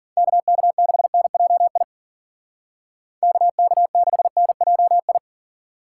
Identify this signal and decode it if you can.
Morse code, heard as a keyed tone, about 53 wpm, 700 Hz. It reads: KK6NJI KK6NJI